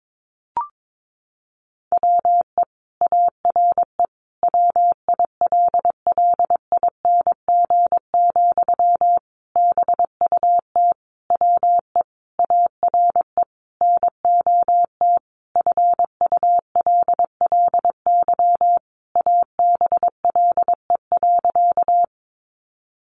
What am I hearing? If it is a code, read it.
Morse code, heard as a keyed tone, about 22 wpm, 700 Hz. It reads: WE ARE WILLING, BUT WE ARE NOT FULLY ABLE.
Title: Sentences from Top 1300 Words 22wpm